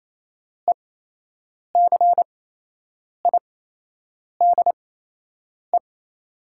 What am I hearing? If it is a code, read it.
Morse code, heard as a keyed tone, about 28 wpm, spaced as for 8 wpm, 700 Hz. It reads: ECIDE
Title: 8x Character Spacing - Top 500 Words - 28wpm